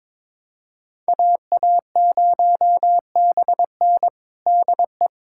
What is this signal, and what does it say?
Morse code, heard as a keyed tone, about 22 wpm, 700 Hz. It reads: AA0BN DE
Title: USA Callsign DE Callsign - 5 Characters - 22wpm